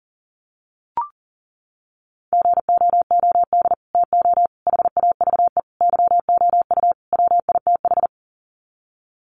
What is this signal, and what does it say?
Morse code, heard as a keyed tone, about 40 wpm, 700 Hz. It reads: GOOD TO HAVE YOU WITH
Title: Sets of 5 Words Encore2 40wpm